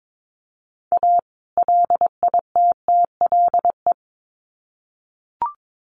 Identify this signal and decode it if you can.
Morse code, heard as a keyed tone, about 22 wpm, 700 Hz. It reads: A LITTLE
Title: Sets of 2 Words 22wpm